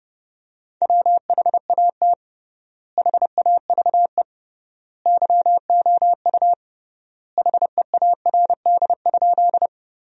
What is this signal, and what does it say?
Morse code, heard as a keyed tone, about 30 wpm, 700 Hz. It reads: WHAT HAVE YOU HEARD?
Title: 3x Word-Spacing - Sentences from Top 400 Words - 30wpm